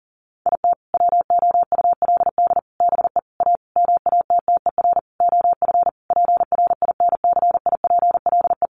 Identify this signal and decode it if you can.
Morse code, heard as a keyed tone, about 40 wpm, 700 Hz. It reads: IT WOULD BE A MATTER OF PRINCIPLE
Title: Sentences from Top 1400 Words 40wpm